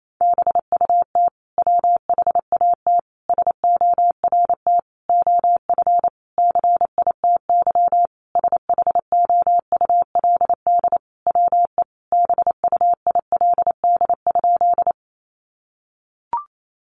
Morse code, heard as a keyed tone, about 28 wpm, 700 Hz. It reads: BUT WHAT SORT OF CITY SHOULD WE BUILD?
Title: Sentences from Top 1300 Words 28wpm